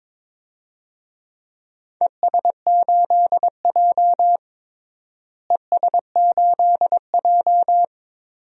Morse code, heard as a keyed tone, about 22 wpm, 700 Hz. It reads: ES8J ES8J
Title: DXCC-100 Call Signs - 4 Characters - Repeated 2x - 22wpm